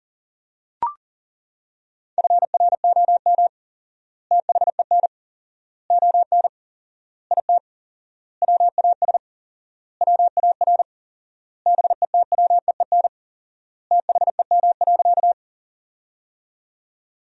Morse code, heard as a keyed tone, about 37 wpm, 700 Hz. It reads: FROM THEN ON IT WAS WAR BETWEEN THEM.
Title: 4x Word-Spacing - Sentences from Top 400 Words - 40wpm